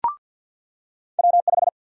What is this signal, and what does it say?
Morse code, heard as a keyed tone, about 48 wpm, 700 Hz. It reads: W5
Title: USA Call Sign Prefixes - 50wpm